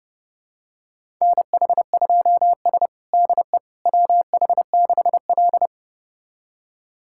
Morse code, heard as a keyed tone, about 31 wpm, 700 Hz. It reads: NH2S DE WH6L
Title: USA Callsign DE Callsign - 4 Characters - 30wpm